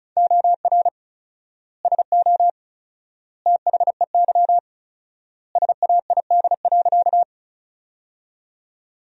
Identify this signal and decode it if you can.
Morse code, heard as a keyed tone, about 34 wpm, 700 Hz. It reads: OR SO THEY SAID.
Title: 4x Word-Spacing - Sentences from Top 100 Words - 35wpm